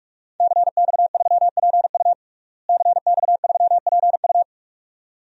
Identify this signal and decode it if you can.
Morse code, heard as a keyed tone, about 47 wpm, 700 Hz. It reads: CX3PU CX3PU
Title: DXCC-100 Call Signs - 5 Characters - Repeated 2x - 45wpm